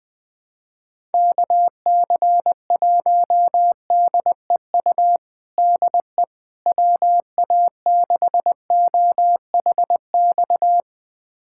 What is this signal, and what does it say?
Morse code, heard as a keyed tone, about 20 wpm, 700 Hz. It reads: KC1DEU DE WA6OHX
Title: USA Callsign DE Callsign - 6 Characters - 20wpm